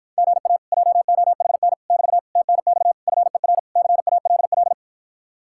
Morse code, heard as a keyed tone, about 53 wpm, 700 Hz. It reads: GA JOHN <BT> TNX FER CALL